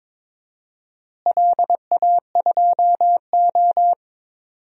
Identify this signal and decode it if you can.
Morse code, heard as a keyed tone, about 22 wpm, 700 Hz. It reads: LA2O
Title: DXCC-100 Call Signs - 4 Characters - Repeated 2x - 22wpm